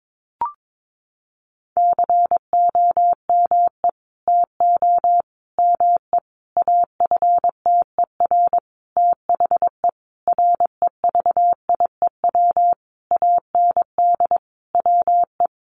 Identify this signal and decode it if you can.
Morse code, heard as a keyed tone, about 22 wpm, 700 Hz. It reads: COME TO ME AFTER THE REVIEW AND WE
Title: Sentences from Top 900 Words 22wpm